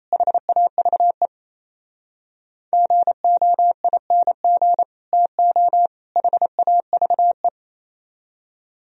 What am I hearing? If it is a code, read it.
Morse code, heard as a keyed tone, about 36 wpm, 700 Hz. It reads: HAVE GOING TO HAVE